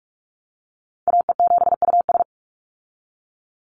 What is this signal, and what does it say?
Morse code, heard as a keyed tone, about 44 wpm, 700 Hz. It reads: AE7US
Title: Call Signs Repeated 3x - 45wpm